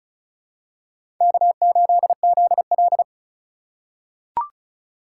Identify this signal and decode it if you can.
Morse code, heard as a keyed tone, about 36 wpm, 700 Hz. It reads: K8ZL
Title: USA Call Signs - 3x Character Spacing - 35wpm